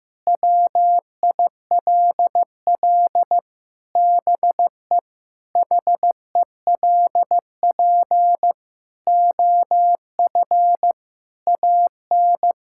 Morse code, heard as a keyed tone, about 15 wpm, 700 Hz. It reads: WILL BE HELP OF AN
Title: Sentences from Top 1000 Words - Review Part 5 - 15wpm